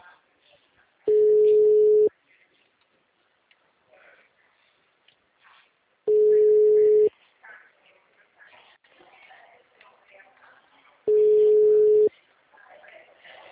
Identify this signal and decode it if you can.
Ringback tone (Europe, 425 Hz), with speech.
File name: out- -2011- - - .25750